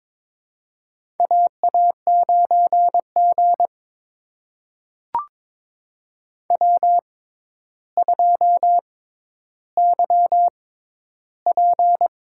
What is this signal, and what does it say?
Morse code, heard as a keyed tone, about 23 wpm, spaced as for 11 wpm, 700 Hz. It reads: AA9G W2YP
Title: USA Call Signs - 6x Character Spacing - 22wpm